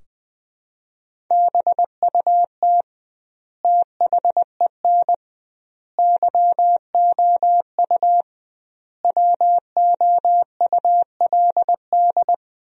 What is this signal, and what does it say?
Morse code, heard as a keyed tone, about 20 wpm, 700 Hz. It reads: BUT THEN YOU WOULD
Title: 2x Word-Spacing - Sentences from Top 300 Words - 20wpm